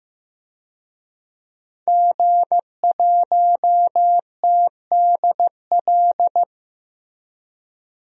Morse code, heard as a keyed tone, about 16 wpm, 700 Hz. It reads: G1TDL
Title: DXCC-100 Call Signs - 5 Characters - 15wpm